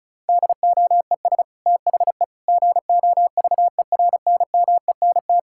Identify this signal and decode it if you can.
Morse code, heard as a keyed tone, about 36 wpm, 700 Hz. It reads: DOES THE GOVERNMENT
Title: Sentences from Top 1600 Words 35wpm